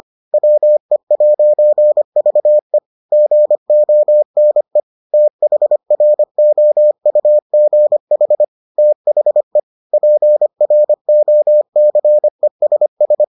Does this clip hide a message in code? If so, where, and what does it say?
Yes: Morse code from 0.3 to 13.2 s, heard as a keyed tone, about 25 wpm, 600 Hz. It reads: WE'VE GONE THROUGH THE PROCESS